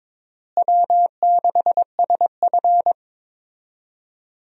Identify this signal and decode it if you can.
Morse code, heard as a keyed tone, about 22 wpm, 700 Hz. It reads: W6SF
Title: USA Call Signs - 4 Characters - Repeated 3x - 22wpm